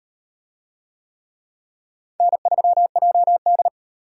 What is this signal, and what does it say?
Morse code, heard as a keyed tone, about 38 wpm, 700 Hz. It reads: N3JD